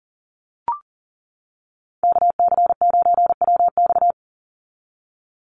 Morse code, heard as a keyed tone, about 40 wpm, 700 Hz. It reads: KC9WX
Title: USA Call Signs 40wpm